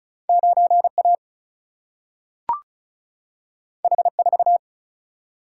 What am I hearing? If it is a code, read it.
Morse code, heard as a keyed tone, about 36 wpm, 700 Hz. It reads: 9A H4
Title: Single-Letter Single-Number 35wpm